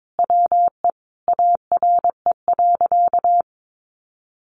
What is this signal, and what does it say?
Morse code, heard as a keyed tone, about 22 wpm, 700 Hz. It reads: WE ARE.
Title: Sentences from Top 1400 Words 22wpm